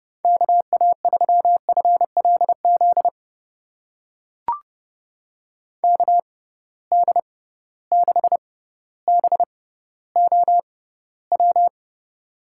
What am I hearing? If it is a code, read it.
Morse code, heard as a keyed tone, about 31 wpm, spaced as for 16 wpm, 700 Hz. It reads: KA3FLZ KD6BOW
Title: USA Call Signs - 6 Characters - 6x Character Spacing - 30wpm